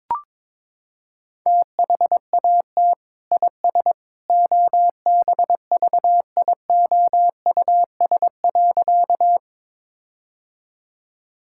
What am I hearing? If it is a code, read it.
Morse code, heard as a keyed tone, about 22 wpm, 700 Hz. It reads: THAT IS OBVIOUS.
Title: Sentences from Top 1600 Words 22wpm